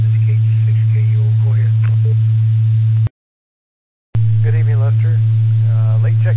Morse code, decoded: none

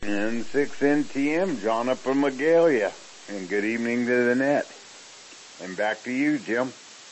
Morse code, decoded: none

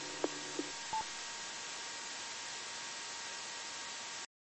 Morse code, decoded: none

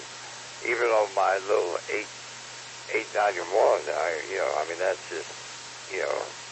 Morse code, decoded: none

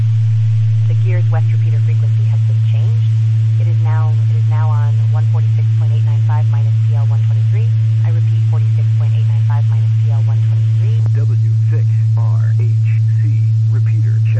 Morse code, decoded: none

